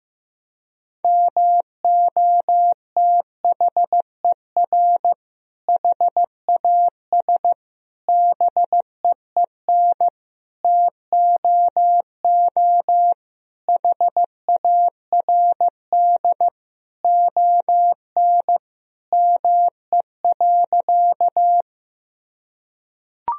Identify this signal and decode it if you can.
Morse code, heard as a keyed tone, about 16 wpm, 700 Hz. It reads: MOTHER HAS BEEN TOO HARD ON ME.
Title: Sentences from Top 1000 Words - Review Part 1 - 15wpm